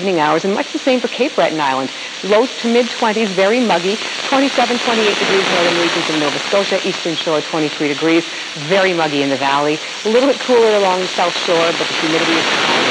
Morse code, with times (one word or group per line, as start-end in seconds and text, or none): none